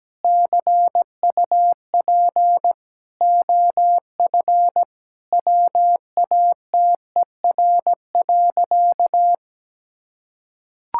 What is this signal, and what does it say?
Morse code, heard as a keyed tone, about 17 wpm, 700 Hz. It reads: CUP OF WATER.
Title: Sentences from Top 1600 Words 17wpm